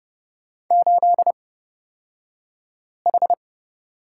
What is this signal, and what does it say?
Morse code, heard as a keyed tone, about 30 wpm, spaced as for 5 wpm, 700 Hz. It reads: 8H
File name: Lesson 36d - BT - Two Characters 30wpm